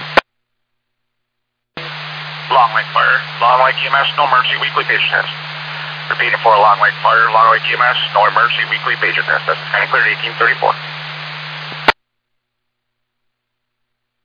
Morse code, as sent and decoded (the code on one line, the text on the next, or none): none